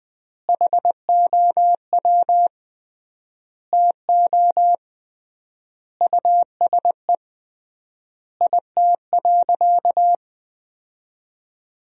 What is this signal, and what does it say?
Morse code, heard as a keyed tone, about 20 wpm, 700 Hz. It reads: HOW TO USE IT.
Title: 3x Word-Spacing - Sentences from Top 500 Words - 20wpm